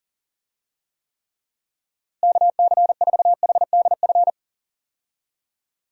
Morse code, heard as a keyed tone, about 40 wpm, 700 Hz. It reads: KC4HDF
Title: USA Call Signs - 6 Characters - Repeated 3x - 40wpm